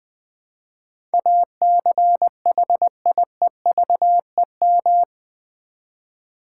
Morse code, heard as a keyed tone, about 20 wpm, 700 Hz. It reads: ACHIEVEM
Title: Build A Word - 4 Letters At A Time - 11-Letter Words - 20wpm